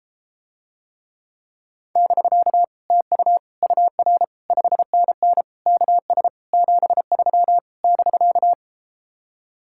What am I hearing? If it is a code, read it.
Morse code, heard as a keyed tone, about 34 wpm, 700 Hz. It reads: <BK> TU UR 5NN KS 73 <BK>